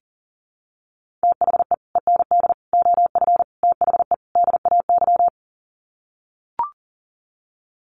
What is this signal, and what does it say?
Morse code, heard as a keyed tone, about 40 wpm, 700 Hz. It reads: THE END OF THE DAY
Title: Sets of 5 Words 40wpm